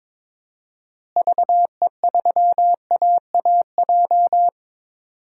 Morse code, heard as a keyed tone, about 22 wpm, 700 Hz. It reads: VE3AAJ
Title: DXCC-100 Call Signs - 6 Characters - Repeated 3x - 22wpm